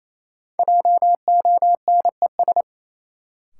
Morse code, heard as a keyed tone, about 28 wpm, 700 Hz. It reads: JONES
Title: Sets of 5 Words - New Word Drill - 28wpm